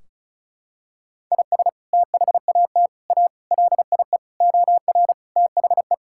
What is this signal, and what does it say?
Morse code, heard as a keyed tone, about 34 wpm, 700 Hz. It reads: IS THAT A LIE OR THE
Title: Sentences from Top 1600 Words 35wpm